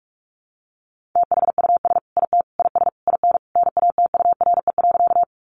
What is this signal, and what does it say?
Morse code, heard as a keyed tone, about 44 wpm, 700 Hz. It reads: THUS IT IS IN NATURE.
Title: Sentences from Top 800 Words 45wpm